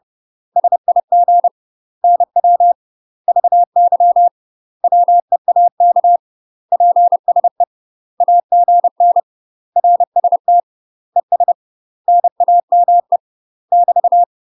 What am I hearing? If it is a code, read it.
Morse code, heard as a keyed tone, about 30 wpm, 700 Hz. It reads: SIG NW VY WEAK PSE AGN RST ES NAME <BT>
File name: UK MASTER QSOs TRAINING / 30WPM / UK MASTER QSOs 0002